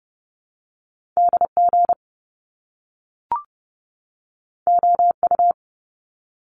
Morse code, heard as a keyed tone, about 30 wpm, 700 Hz. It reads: DG OU
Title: Top Two Letter Combinations in Words 30wpm